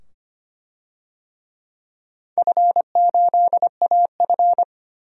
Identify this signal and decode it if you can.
Morse code, heard as a keyed tone, about 25 wpm, 700 Hz. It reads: F8AF
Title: DXCC-100 Call Signs - 4 Characters - 25wpm